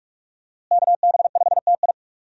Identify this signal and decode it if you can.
Morse code, heard as a keyed tone, about 47 wpm, 700 Hz. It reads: KB5TI
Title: USA Call Signs - 5 Characters - Repeated 2x - 45wpm